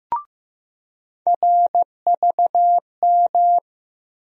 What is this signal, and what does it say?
Morse code, heard as a keyed tone, about 16 wpm, 700 Hz. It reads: RVM
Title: USA Call Sign Postfixes - 3 Letters 15wpm